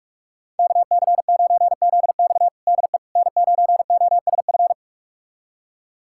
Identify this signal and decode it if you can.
Morse code, heard as a keyed tone, about 46 wpm, 700 Hz. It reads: KC9ZX DE N9OSF
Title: USA Callsign DE Callsign - 5 Characters - 45wpm